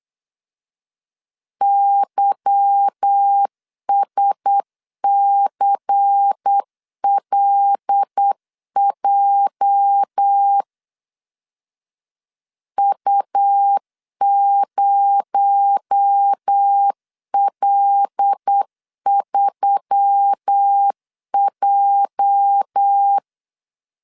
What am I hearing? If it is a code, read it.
Morse code, heard as a keyed tone, about 8 wpm, 800 Hz. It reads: YSCLJ U0L3J
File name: LEZIONI / 8 YJ45 / 8 YJ 8wpm 3313